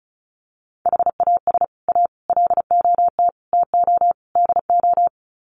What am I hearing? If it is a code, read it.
Morse code, heard as a keyed tone, about 34 wpm, 700 Hz. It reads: HAS A LOT TO DO
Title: Sets of 5 Words Encore2 35wpm